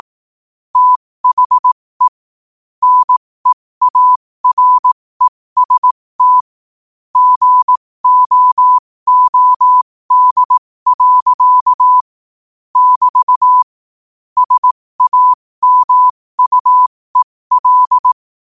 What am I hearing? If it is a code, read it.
Morse code, heard as a keyed tone, about 18 wpm, spaced as for 15 wpm, 1000 Hz. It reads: THE NEAREST GOOD. <BT> SAMUEL
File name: qotdfiles / 15wpm